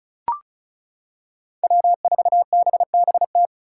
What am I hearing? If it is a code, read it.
Morse code, heard as a keyed tone, about 36 wpm, 700 Hz. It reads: W4BBT
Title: K1USN SST QSO Practice - 35wpm